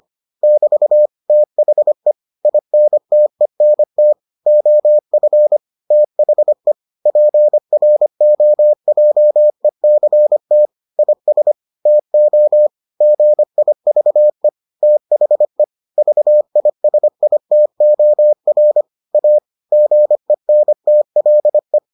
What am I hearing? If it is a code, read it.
Morse code, heard as a keyed tone, about 25 wpm, 600 Hz. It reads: <BT> THE INTENT OF THE PROJECT IS TO GIVE THE VISITOR A GENTLE